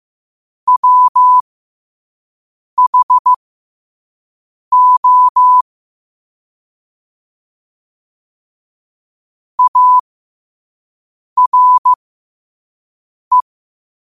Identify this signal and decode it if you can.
Morse code, heard as a keyed tone, about 15 wpm, spaced as for 5 wpm, 1000 Hz. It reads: WHO ARE
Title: 5 WPM morse code quotes for Fri, 01 Mar 2024